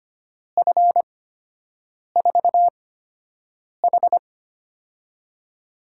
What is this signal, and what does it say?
Morse code, heard as a keyed tone, about 25 wpm, spaced as for 7 wpm, 700 Hz. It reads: F4H
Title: USA Call Signs - 4 Characters - 8x Character Spacing - 25wpm